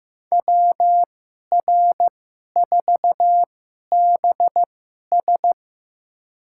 Morse code, heard as a keyed tone, about 15 wpm, spaced as for 11 wpm, 700 Hz. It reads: WR4BS
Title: USA Call Signs - 5 Characters - 2x Character Spacing - 15wpm